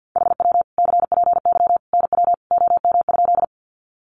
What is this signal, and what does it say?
Morse code, heard as a keyed tone, about 50 wpm, 700 Hz. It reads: HW CPY NW OM?
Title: Common QSO Phrases 50wpm